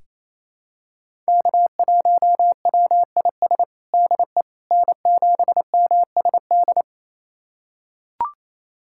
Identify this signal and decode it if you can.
Morse code, heard as a keyed tone, about 28 wpm, 700 Hz. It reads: K1WIS DE N7MSD
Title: USA Callsign DE Callsign 28wpm